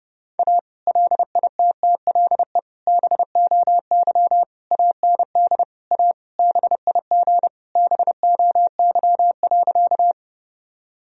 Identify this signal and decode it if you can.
Morse code, heard as a keyed tone, about 30 wpm, 700 Hz. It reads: A LITTLE BOY AND A BIG BOY.
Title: Sentences from Top 1000 Words - Review Part 1 - 30wpm